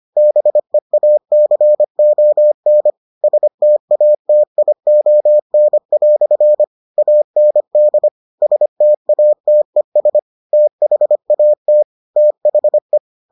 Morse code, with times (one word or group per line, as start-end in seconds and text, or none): 0.2-2.9 BEACON
3.2-6.6 STATION"
7.0-8.1 AND
8.4-10.2 STATES
10.5-11.8 THAT
12.2-13.0 THE